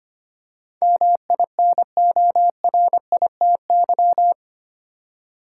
Morse code, heard as a keyed tone, about 25 wpm, 700 Hz. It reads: MINORITY
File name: 8 Letter Words - First 3 Letters Given - 25wpm